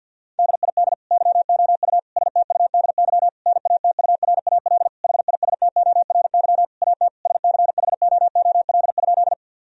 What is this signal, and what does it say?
Morse code, heard as a keyed tone, about 50 wpm, 700 Hz. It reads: DID YOU STUDY NATURAL HISTORY AT SCHOOL?
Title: Sentences from Top 1000 Words - Review Part 4 - 50wpm